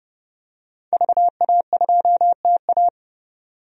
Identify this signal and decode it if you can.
Morse code, heard as a keyed tone, about 30 wpm, 700 Hz. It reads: VA2TA
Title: DXCC-100 Call Signs - 5 Characters - 30wpm